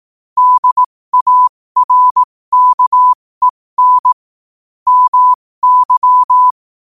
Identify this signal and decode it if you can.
Morse code, heard as a keyed tone, about 18 wpm, spaced as for 15 wpm, 1000 Hz. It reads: DARKEN MY
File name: qotdfiles / 15wpm